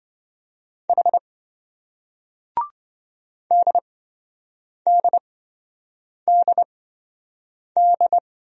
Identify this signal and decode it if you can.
Morse code, heard as a keyed tone, about 27 wpm, spaced as for 7 wpm, 700 Hz. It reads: H DDDD